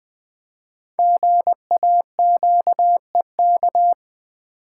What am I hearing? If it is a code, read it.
Morse code, heard as a keyed tone, about 20 wpm, 700 Hz. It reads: GAQEK
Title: Code Groups - 5 Letters - 20wpm